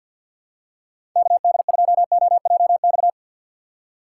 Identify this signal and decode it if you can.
Morse code, heard as a keyed tone, about 51 wpm, 690 Hz. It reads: KD2OJX
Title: USA Call Signs - 3x Character Spacing - 50wpm